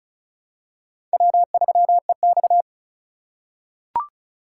Morse code, heard as a keyed tone, about 34 wpm, 700 Hz. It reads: W3EX
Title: USA Call Signs - 4 Characters - 2x Character Spacing - 35wpm